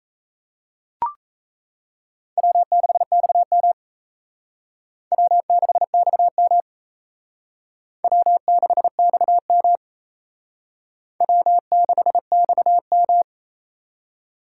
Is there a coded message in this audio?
Yes: Morse code from 2.4 to 13.2 s, heard as a keyed tone, about 35 wpm, 700 Hz. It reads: W6XM W6XM W6XM W6XM